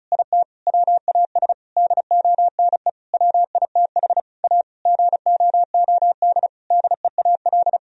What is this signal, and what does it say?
Morse code, heard as a keyed tone, about 36 wpm, 700 Hz. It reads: IT WAS DONE WITH A GOOD DEAL
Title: Sentences from Top 1000 Words - Review Part 4 - 35wpm